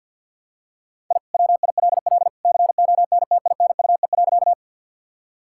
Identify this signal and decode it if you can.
Morse code, heard as a keyed tone, about 50 wpm, 700 Hz. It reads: I WILL CONTINUE.
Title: Sentences from Top 1000 Words - Review Part 6 - 50wpm